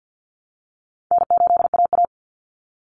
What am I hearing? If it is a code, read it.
Morse code, heard as a keyed tone, about 50 wpm, 700 Hz. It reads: N8AA
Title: Easy Call Signs Repeated 3x - 50wpm